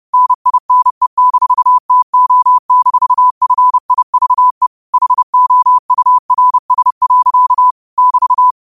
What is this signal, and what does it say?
Morse code, heard as a keyed tone, about 30 wpm, 1000 Hz. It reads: NINE-TO-FIVE HOURS. <BT>